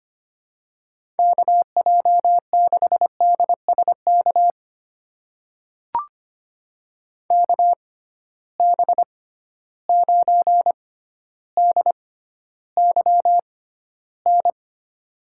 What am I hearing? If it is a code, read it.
Morse code, heard as a keyed tone, about 25 wpm, spaced as for 13 wpm, 700 Hz. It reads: KJ6DSK KB9DYN
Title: USA Call Signs - 6 Characters - 6x Character Spacing - 25wpm